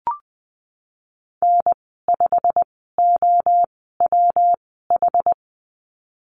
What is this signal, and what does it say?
Morse code, heard as a keyed tone, about 20 wpm, spaced as for 15 wpm, 700 Hz. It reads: N5OWH
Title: 122 Lesson 20g - Number 3 - Call Signs 20wpm